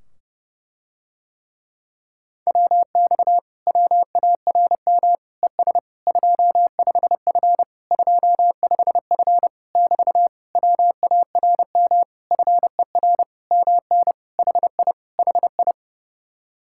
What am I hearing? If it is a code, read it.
Morse code, heard as a keyed tone, about 32 wpm, 700 Hz. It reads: WX WARM ES 25F 25F <BT> WARM FER MN HI HI